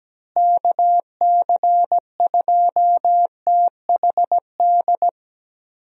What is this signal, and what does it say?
Morse code, heard as a keyed tone, about 17 wpm, 700 Hz. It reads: KC2THD
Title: USA Call Signs - 6 Characters - Repeated 3x - 17wpm